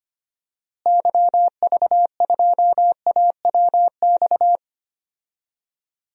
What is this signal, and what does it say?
Morse code, heard as a keyed tone, about 25 wpm, 700 Hz. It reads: YV2AWX
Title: DXCC-100 Call Signs - 6 Characters - Repeated 2x - 25wpm